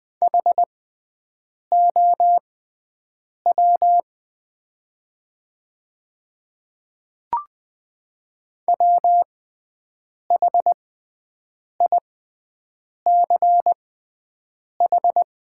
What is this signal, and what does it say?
Morse code, heard as a keyed tone, about 20 wpm, spaced as for 7 wpm, 700 Hz. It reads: HOW WHICH
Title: 6x Character Spacing - Top 100 Words - 20wpm